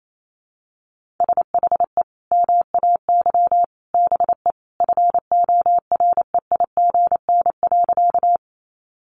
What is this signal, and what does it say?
Morse code, heard as a keyed tone, about 28 wpm, 700 Hz. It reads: SHE MAY BE FOREIGN.
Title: Sentences from Top 1400 Words 28wpm